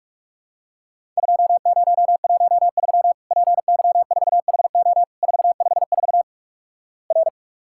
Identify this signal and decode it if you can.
Morse code, heard as a keyed tone, about 46 wpm, 700 Hz. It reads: 2013 PY4HO 454 R
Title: Worked All Europe DX Contest - QTC Exchange - 45wpm